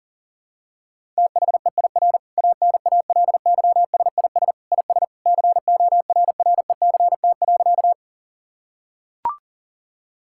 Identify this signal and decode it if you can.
Morse code, heard as a keyed tone, about 40 wpm, 700 Hz. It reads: THEIR ANALYSIS IS CORRECT.
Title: Sentences from Top 1000 Words - Review Part 5 - 40wpm